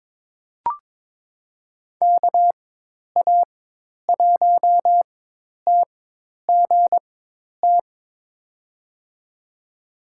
Morse code, heard as a keyed tone, about 22 wpm, spaced as for 10 wpm, 700 Hz. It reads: KA1TGT
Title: USA Call Signs - 6 Characters - 4x Character Spacing - 22wpm